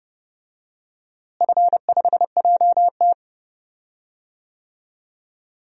Morse code, heard as a keyed tone, about 30 wpm, 700 Hz. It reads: F5JT
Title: DXCC-100 Call Signs - 4 Characters - Repeated 3x - 30wpm